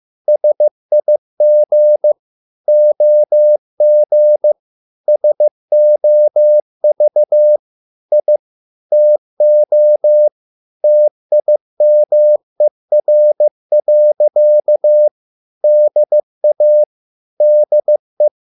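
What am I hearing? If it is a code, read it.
Morse code, heard as a keyed tone, about 15 wpm, 600 Hz. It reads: SIG OG SOV I TO TIMER. DA DE
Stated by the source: Morse code at a slow 15 wpm from CW med Gnister.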